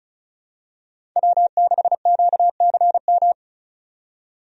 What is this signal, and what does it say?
Morse code, heard as a keyed tone, about 34 wpm, 700 Hz. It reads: W6QCM
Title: USA Call Signs - 5 Characters - 35wpm